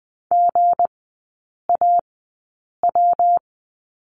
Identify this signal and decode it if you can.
Morse code, heard as a keyed tone, about 20 wpm, spaced as for 8 wpm, 690 Hz. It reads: GAW
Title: 166 Lesson 27e - Number 7 - Three Characters 20wpm